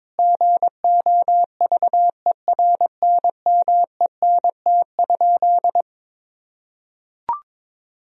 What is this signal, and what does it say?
Morse code, heard as a keyed tone, about 22 wpm, 700 Hz. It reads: GOVERNMENT?
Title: Sentences from Top 1000 Words - Review Part 4 - 22wpm